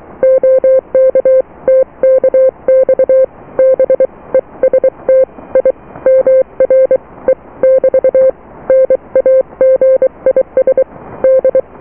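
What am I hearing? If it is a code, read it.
Morse code, heard as a keyed tone, about 23 wpm, 520 Hz. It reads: OK TKX BEST I MR E <BT> NAGIS D
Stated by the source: Morse QSO on 14 MHz with a German station, received at RST 579.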